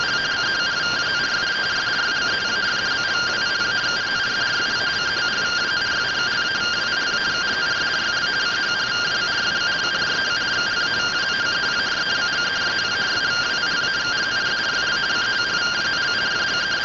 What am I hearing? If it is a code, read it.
Two-tone teleprinter signal, about 170 Hz shift.
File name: findings / AXIAL fr1 3mm.gen-10 rtty rx